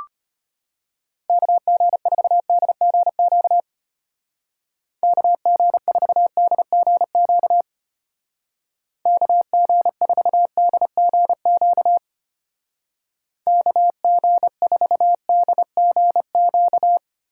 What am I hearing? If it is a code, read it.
Morse code, heard as a keyed tone, about 32 wpm, 700 Hz. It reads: KG4DGQ KG4DGQ KG4DGQ KG4DGQ